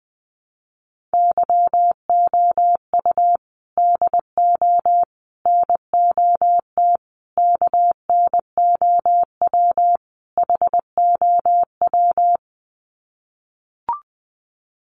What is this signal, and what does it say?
Morse code, heard as a keyed tone, about 20 wpm, 700 Hz. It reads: YOU DO NOT KNOW HOW
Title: Sets of 5 Words Encore 20wpm